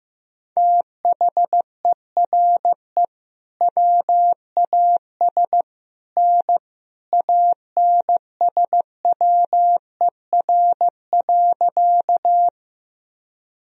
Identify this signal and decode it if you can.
Morse code, heard as a keyed tone, about 15 wpm, 700 Hz. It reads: THERE WAS N ANSWER.
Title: Sentences from Top 300 Words and Abbreviations 15wpm